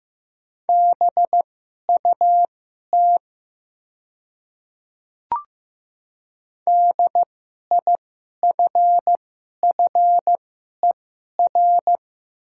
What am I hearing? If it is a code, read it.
Morse code, heard as a keyed tone, about 15 wpm, spaced as for 11 wpm, 700 Hz. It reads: BUT DIFFER
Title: 2x Character Spacing - Top 300 Words - 15wpm